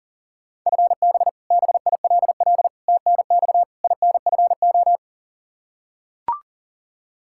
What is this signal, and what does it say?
Morse code, heard as a keyed tone, about 40 wpm, 700 Hz. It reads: FB BILL TNX INFO
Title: Practice QSO - Exchange 3 - More Info - 40wpm